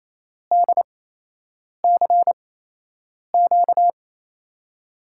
Morse code, heard as a keyed tone, about 28 wpm, spaced as for 8 wpm, 700 Hz. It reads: DCQ
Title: USA Call Signs - 5 Characters - 8x Character Spacing - 28wpm